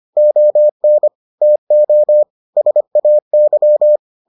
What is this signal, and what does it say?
Morse code, heard as a keyed tone, about 25 wpm, 600 Hz. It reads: ON TO SAY